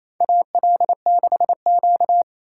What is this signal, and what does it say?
Morse code, heard as a keyed tone, about 28 wpm, 700 Hz. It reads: AL6Q
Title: USA Call Signs - 4 Characters - 4x Character Spacing - 28wpm